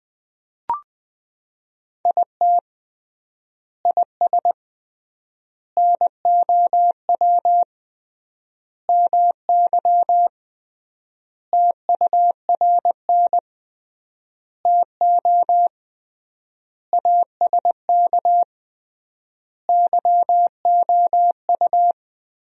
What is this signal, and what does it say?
Morse code, heard as a keyed tone, about 20 wpm, 700 Hz. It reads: IT IS NOW MY TURN TO ASK YOU
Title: 3x Word-Spacing - Sentences from Top 200 Words - 20wpm